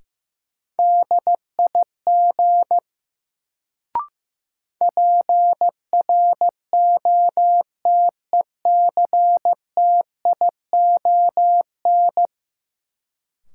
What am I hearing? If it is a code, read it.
Morse code, heard as a keyed tone, about 15 wpm, 700 Hz. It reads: DIG PROTECTION
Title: Sentences from Top 1600 Words - New Word Drill Alt - 15wpm